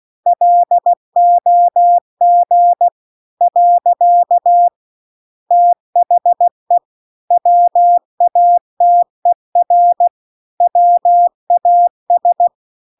Morse code, heard as a keyed tone, about 16 wpm, 700 Hz. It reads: LOG . THE WATER WAS